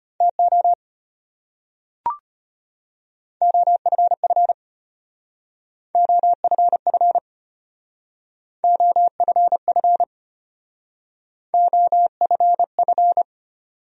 Morse code, heard as a keyed tone, about 32 wpm, 700 Hz. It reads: TO OFF OFF OFF OFF